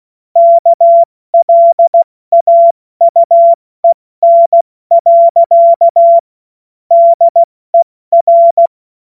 Morse code, heard as a keyed tone, about 16 wpm, 680 Hz. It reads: KLAUEN. DER
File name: Morsekurs / Witze / Witz 28.November 2023 70 F80